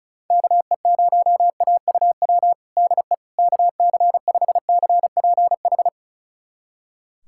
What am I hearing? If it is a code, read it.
Morse code, heard as a keyed tone, about 36 wpm, 690 Hz. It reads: KE0AUW DE KC5CPH
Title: USA Callsign DE Callsign - 6 Characters - 35wpm